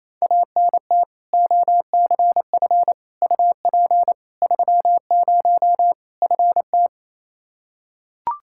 Morse code, heard as a keyed tone, about 28 wpm, 700 Hz. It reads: ANT OCF UP 30 FT
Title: Practice QSO - Exchange 2 - Info - 28wpm